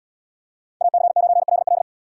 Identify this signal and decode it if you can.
Morse code, heard as a keyed tone, about 77 wpm, 700 Hz. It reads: AC9LL